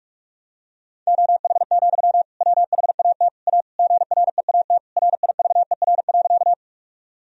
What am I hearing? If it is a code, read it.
Morse code, heard as a keyed tone, about 46 wpm, 700 Hz. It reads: OH, WHAT A GREAT RIVER.
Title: Sentences from Top 1000 Words - Review Part 1 - 45wpm